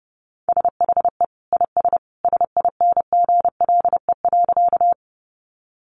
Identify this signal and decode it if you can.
Morse code, heard as a keyed tone, about 30 wpm, 700 Hz. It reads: SHE IS SINGLE.
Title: Sentences from Top 700 Words 30wpm